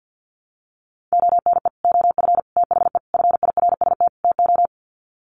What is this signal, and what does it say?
Morse code, heard as a keyed tone, about 50 wpm, 700 Hz. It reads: ONE OF THE FIRST TO
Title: Sets of 5 Words Encore 50wpm